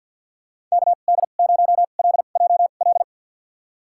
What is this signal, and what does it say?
Morse code, heard as a keyed tone, about 52 wpm, spaced as for 37 wpm, 700 Hz. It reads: KD0LJL
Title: USA Call Signs - 2x Character Spacing - 50wpm